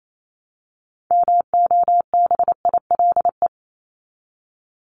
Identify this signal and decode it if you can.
Morse code, heard as a keyed tone, about 28 wpm, 700 Hz. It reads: MOBILE
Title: Top 1751-2000 Words 28wpm